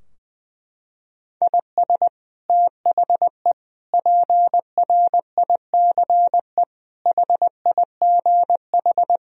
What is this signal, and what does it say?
Morse code, heard as a keyed tone, about 20 wpm, 700 Hz. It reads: IS THE PRICE HIGH
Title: Sentences from Top 1000 Words - Review Part 4 - 20wpm